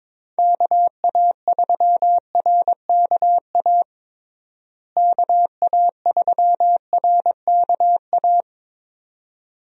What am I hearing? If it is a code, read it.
Morse code, heard as a keyed tone, about 21 wpm, 700 Hz. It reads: KA3RKA KA3RKA
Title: USA Call Signs - 6 Characters - Repeated 2x - 22wpm